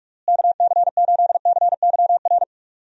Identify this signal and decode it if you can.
Morse code, heard as a keyed tone, about 47 wpm, 700 Hz. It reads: KC8CYR
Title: USA Call Signs - 6 Characters - Repeated 3x - 45wpm